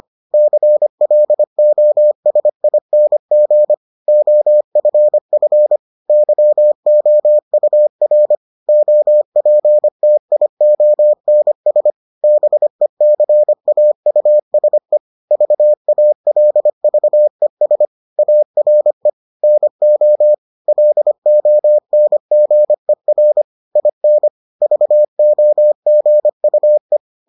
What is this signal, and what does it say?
Morse code, heard as a keyed tone, about 25 wpm, 600 Hz. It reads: CLOSING OFF YOUR OPTIONS BECAUSE VALVES ARE NO LONGER IN VOGUE